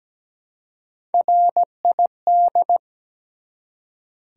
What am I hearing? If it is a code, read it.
Morse code, heard as a keyed tone, about 17 wpm, 700 Hz. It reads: RID
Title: Three Letter Words - 17wpm